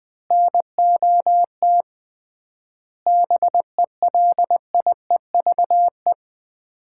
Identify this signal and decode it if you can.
Morse code, heard as a keyed tone, about 20 wpm, 700 Hz. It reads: NOT BELIEVE
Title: 3x Word-Spacing - Sets of 4 Words - 20wpm